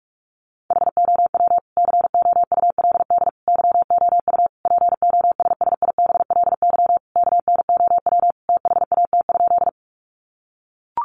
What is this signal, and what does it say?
Morse code, heard as a keyed tone, about 44 wpm, 700 Hz. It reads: HOW COULD YOU POSSIBLY KNOW THAT?
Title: Sentences from Top 1300 Words 45wpm